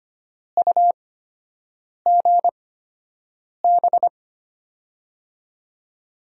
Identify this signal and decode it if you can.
Morse code, heard as a keyed tone, about 25 wpm, spaced as for 7 wpm, 700 Hz. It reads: UGB
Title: USA Call Signs - 6 Characters - 8x Character Spacing - 25wpm